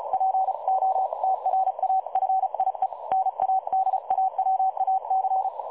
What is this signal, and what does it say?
Morse code, heard as a keyed tone, about 35 wpm, 760 Hz. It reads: POTRAFIE NADAWAC
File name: dlazaawansowanych / nagraniazpasma / websdr recording 3565.0kHz